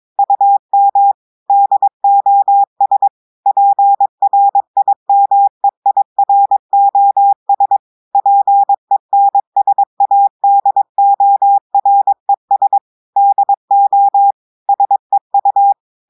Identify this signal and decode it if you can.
Morse code, heard as a keyed tone, about 23 wpm, 800 Hz. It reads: UM DOS PRIMEIROS PENSADORES DO SEU
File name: audios / portugais / portugais texte en clair 1 vitesse 22 mots minutes